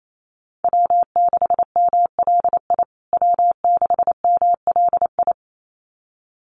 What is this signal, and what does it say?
Morse code, heard as a keyed tone, about 28 wpm, 700 Hz. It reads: W6MLI W6MLI
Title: USA Call Signs Repeated 2x 28wpm